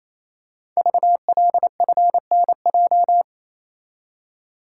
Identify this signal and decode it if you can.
Morse code, heard as a keyed tone, about 29 wpm, 690 Hz. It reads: VLFNJ
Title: Code Groups - 5 Letters - 28wpm